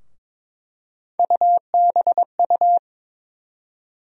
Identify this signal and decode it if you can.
Morse code, heard as a keyed tone, about 21 wpm, 700 Hz. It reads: UBU
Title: USA Call Sign Postfixes - 3 Letters 22wpm